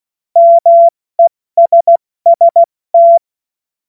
Morse code, heard as a keyed tone, about 16 wpm, 680 Hz. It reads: MESST
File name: Morsekurs / Witze / Witz 16. Dezember 2025 70 F80